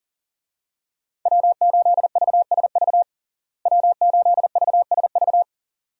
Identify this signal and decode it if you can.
Morse code, heard as a keyed tone, about 40 wpm, 700 Hz. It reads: W8VSV W8VSV
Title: USA Call Signs - 5 Characters - Repeated 2x - 40wpm